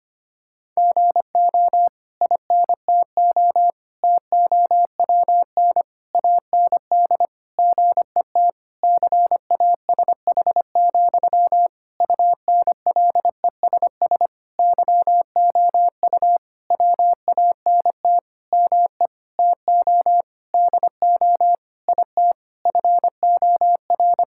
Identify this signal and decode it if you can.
Morse code, heard as a keyed tone, about 25 wpm, 700 Hz. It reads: GO INTO TOWN AND GET CASH, UNLESS YOU WANT ME TO DO IT FOR
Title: Sentences from Top 1000 Words - Review Part 6 - 25wpm